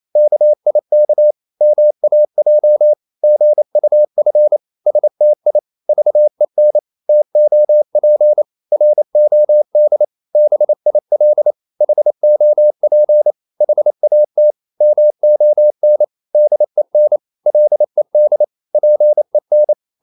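Morse code, heard as a keyed tone, about 28 wpm, 600 Hz. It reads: KIK MAJ GUF STI VEN TOP ROD BIL HOP HAT MON DEN LED PEN
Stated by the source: Danish short words sent in Morse at 28 wpm, from CW med Gnister.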